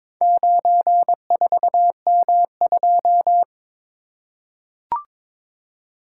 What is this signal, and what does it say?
Morse code, heard as a keyed tone, about 22 wpm, 700 Hz. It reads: 94M2
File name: practice / 22wpm / Four Letter-Number Combinations - 22wpm